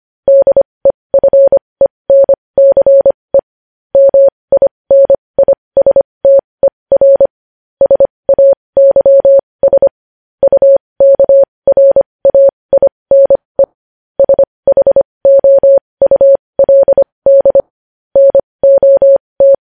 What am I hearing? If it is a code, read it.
Morse code, heard as a keyed tone, about 25 wpm, spaced as for 20 wpm, 550 Hz. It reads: DEFENCE MINISTER SAYS UKRAINE SHOULD NOT
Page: news headlines in morse code at 20 wpm